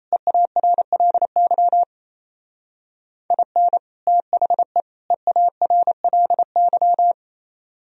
Morse code, heard as a keyed tone, about 32 wpm, 700 Hz. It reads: EARLY IN THE EARLY
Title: Speed-Racing - Sets of 3 Words 28wpm